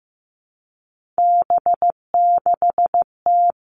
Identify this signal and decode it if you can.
Morse code, heard as a keyed tone, about 15 wpm, 700 Hz. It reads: B6T
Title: Three Challenging Letter-Number Combinations 15wpm